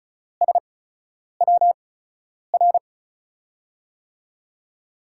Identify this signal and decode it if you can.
Morse code, heard as a keyed tone, about 36 wpm, spaced as for 10 wpm, 700 Hz. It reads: SWR
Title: USA Call Signs - 5 Characters - 8x Character Spacing - 35wpm